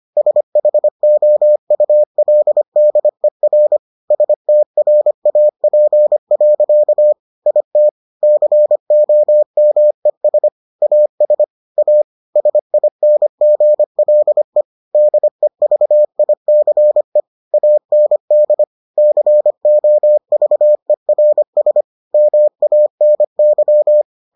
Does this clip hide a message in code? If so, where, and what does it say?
Yes: Morse code from 0.2 to 24.0 s, heard as a keyed tone, about 25 wpm, 600 Hz. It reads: SHOULDER STRAP. IT COMES AS A SINGLE DEVICE AND COVERS MANY